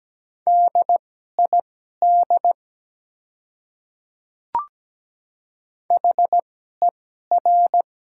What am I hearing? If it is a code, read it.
Morse code, heard as a keyed tone, about 17 wpm, spaced as for 12 wpm, 700 Hz. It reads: DID HER
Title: 2x Character Spacing - Top 100 Words - 17wpm